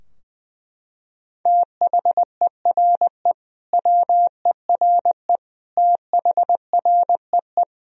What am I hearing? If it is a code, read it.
Morse code, heard as a keyed tone, about 20 wpm, 700 Hz. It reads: THERE WERE THREE
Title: Sentences from Top 1000 Words - Review Part 2 - 20wpm